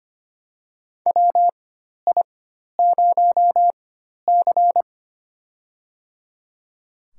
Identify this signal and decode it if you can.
Morse code, heard as a keyed tone, about 25 wpm, spaced as for 12 wpm, 700 Hz. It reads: WI0C
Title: USA Call Signs - 4 Characters - 4x Character Spacing - 25wpm